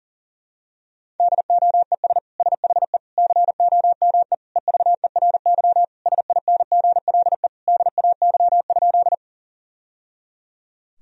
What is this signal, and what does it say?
Morse code, heard as a keyed tone, about 40 wpm, 700 Hz. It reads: DOES SHE COME EVERY SINGLE DAY?
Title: Sentences from Top 1000 Words - Review Part 4 - 40wpm